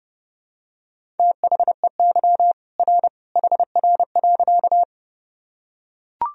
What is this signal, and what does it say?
Morse code, heard as a keyed tone, about 30 wpm, 700 Hz. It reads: THEY R HR.
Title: Sentences from Top 200 Words and Abbreviations 30wpm